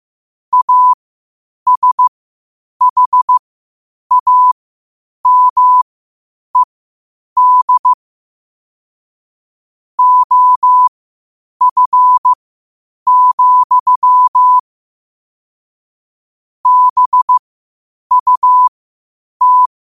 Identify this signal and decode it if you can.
Morse code, heard as a keyed tone, about 15 wpm, spaced as for 8 wpm, 1000 Hz. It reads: ASHAMED OF, BUT
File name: qotdfiles / 8wpm